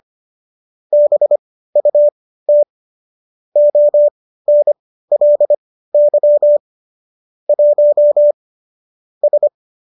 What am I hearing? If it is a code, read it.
Morse code, heard as a keyed tone, about 25 wpm, spaced as for 15 wpm, 590 Hz. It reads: BUT ONLY 1 S